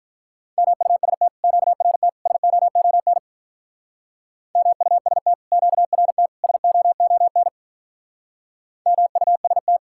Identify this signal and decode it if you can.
Morse code, heard as a keyed tone, about 46 wpm, 700 Hz. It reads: MUST QRT SOON MUST QRT SOON MUST